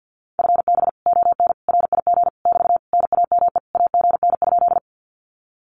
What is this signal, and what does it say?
Morse code, heard as a keyed tone, about 50 wpm, 700 Hz. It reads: FB ON RIG <BT> NAME AGN?
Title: Common QSO Phrases Encore 50wpm